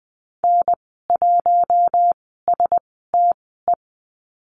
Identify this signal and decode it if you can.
Morse code, heard as a keyed tone, about 20 wpm, spaced as for 14 wpm, 700 Hz. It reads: N1STE
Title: 042 Lesson 8g - Number 1 - Call Signs 20wpm